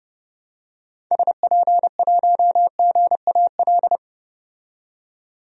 Morse code, heard as a keyed tone, about 30 wpm, 690 Hz. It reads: SP1GAL
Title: DXCC-100 Call Signs - 6 Characters - 30wpm